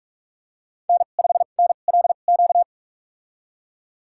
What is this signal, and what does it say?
Morse code, heard as a keyed tone, about 41 wpm, spaced as for 31 wpm, 700 Hz. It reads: N5NLQ
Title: USA Call Signs - 5 Characters - 2x Character Spacing - 45wpm